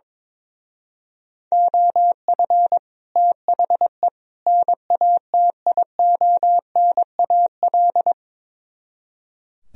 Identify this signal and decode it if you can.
Morse code, heard as a keyed tone, about 22 wpm, 700 Hz. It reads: OF THE NATIONAL
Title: Speed-Racing - Sets of 3 Words 22wpm